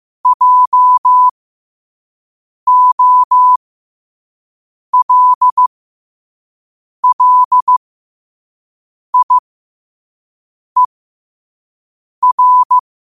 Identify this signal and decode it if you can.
Morse code, heard as a keyed tone, about 15 wpm, spaced as for 5 wpm, 1000 Hz. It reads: JOLLIER
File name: qotdfiles / 5wpm